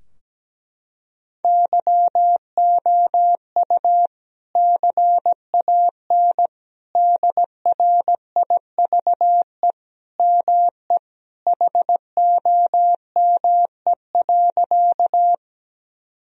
Morse code, heard as a keyed tone, about 17 wpm, 700 Hz. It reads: YOU CAN DRIVE ME HOME.
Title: Sentences from Top 1000 Words - Review Part 2 - 17wpm